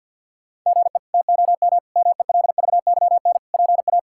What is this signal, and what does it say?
Morse code, heard as a keyed tone, about 50 wpm, 690 Hz. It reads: GE TOM MELVYN PA
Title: K1USN SST QSO Practice - 50wpm